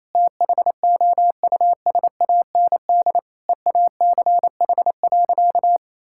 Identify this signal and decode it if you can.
Morse code, heard as a keyed tone, about 28 wpm, 700 Hz. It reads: THOUSAND EACH.
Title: Sentences from Top 1000 Words - Review Part 5 - 28wpm